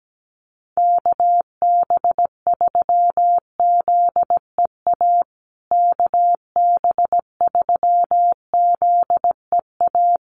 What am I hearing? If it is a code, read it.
Morse code, heard as a keyed tone, about 17 wpm, 700 Hz. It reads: KB3ZEA KB3ZEA
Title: USA Call Signs Repeated 3x 17wpm